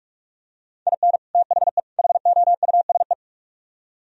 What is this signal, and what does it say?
Morse code, heard as a keyed tone, about 47 wpm, 700 Hz. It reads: IN THE HOUSE